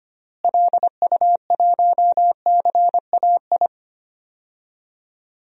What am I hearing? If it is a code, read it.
Morse code, heard as a keyed tone, about 25 wpm, 700 Hz. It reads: LU1CAI
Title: DXCC-100 Call Signs - 6 Characters - Repeated 3x - 25wpm